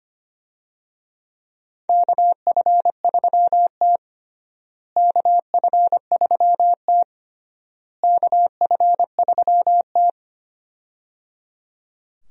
Morse code, heard as a keyed tone, about 25 wpm, 700 Hz. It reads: KF3T KF3T KF3T
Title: USA Call Signs - 4 Characters - Repeated 3x - 25wpm